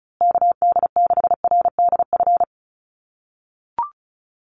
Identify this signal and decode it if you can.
Morse code, heard as a keyed tone, about 34 wpm, 700 Hz. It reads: KD6RDF
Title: USA Call Signs Repeated 3x 35wpm